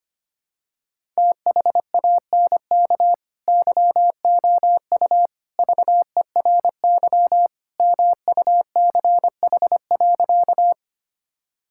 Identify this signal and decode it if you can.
Morse code, heard as a keyed tone, about 25 wpm, 700 Hz. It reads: THANK YOU VERY MUCH.
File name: practice / 25wpm / Sentences with 3-Word Phrases - 25wpm